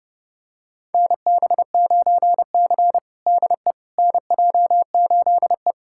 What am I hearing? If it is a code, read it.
Morse code, heard as a keyed tone, about 30 wpm, 700 Hz. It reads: NB9C DE NJ8E
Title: USA Callsign DE Callsign - 4 Characters - 30wpm